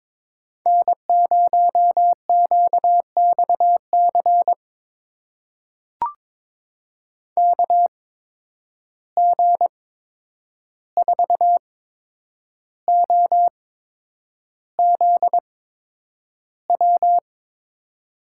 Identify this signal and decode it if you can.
Morse code, heard as a keyed tone, about 22 wpm, spaced as for 9 wpm, 700 Hz. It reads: N0QXC KG4OZW
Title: USA Call Signs - 8x Character Spacing - 22wpm